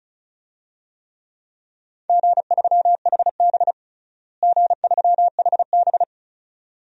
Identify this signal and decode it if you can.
Morse code, heard as a keyed tone, about 36 wpm, 700 Hz. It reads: G3HB G3HB
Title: DXCC-100 Call Signs - 4 Characters - Repeated 2x - 35wpm